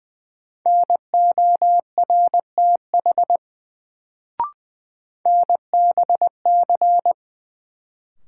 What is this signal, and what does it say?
Morse code, heard as a keyed tone, about 21 wpm, 700 Hz. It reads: NORTH NBC
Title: Sets of 5 Words Encore - New Word Drill - 20wpm